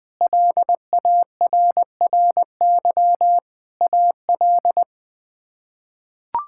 Morse code, heard as a keyed tone, about 20 wpm, 700 Hz. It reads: LARRY AL
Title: K1USN SST QSO Practice - 20wpm